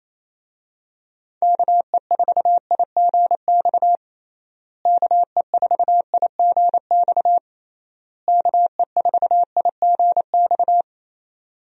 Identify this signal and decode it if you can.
Morse code, heard as a keyed tone, about 28 wpm, 700 Hz. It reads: KE4IGX KE4IGX KE4IGX
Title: USA Call Signs - 6 Characters - Repeated 3x - 28wpm